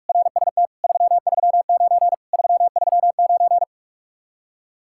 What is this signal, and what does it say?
Morse code, heard as a keyed tone, about 59 wpm, spaced as for 51 wpm, 700 Hz. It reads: RST 339 339
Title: Speed-Racing Common QSO Phrases - 45wpm